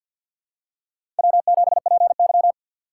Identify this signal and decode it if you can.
Morse code, heard as a keyed tone, about 53 wpm, 700 Hz. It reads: W7PY
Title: USA Call Signs - 4 Characters - 6x Character Spacing - 50wpm